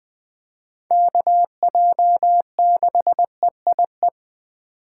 Morse code, heard as a keyed tone, about 21 wpm, 700 Hz. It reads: KJ6EIE
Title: USA Call Signs - 2x Character Spacing - 20wpm